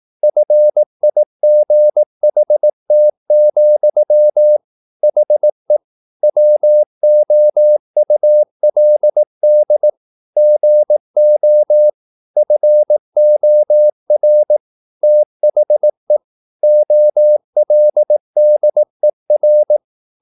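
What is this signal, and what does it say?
Morse code, heard as a keyed tone, about 18 wpm, 600 Hz. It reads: FIGHT, HE WOULD GO FOR THE OLDER
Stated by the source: Morse practice at 18 wpm, English text, from CW med Gnister.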